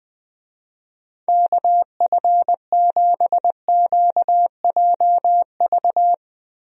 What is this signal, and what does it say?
Morse code, heard as a keyed tone, about 20 wpm, 700 Hz. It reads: KF7QJV
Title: USA Call Signs - 6 Characters - Repeated 3x - 20wpm